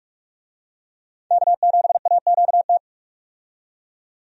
Morse code, heard as a keyed tone, about 45 wpm, 700 Hz. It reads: K7AQT
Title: USA Call Signs - 5 Characters - 45wpm